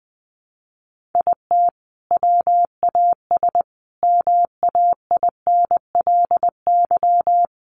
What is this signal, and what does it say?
Morse code, heard as a keyed tone, about 20 wpm, 700 Hz. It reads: IT WAS MAINLY
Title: Sentences from Top 1400 Words 20wpm